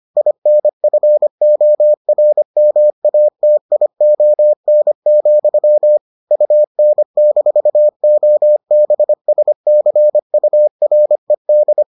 Morse code, heard as a keyed tone, about 25 wpm, 600 Hz. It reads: INFORMATION, UN-OBSCURED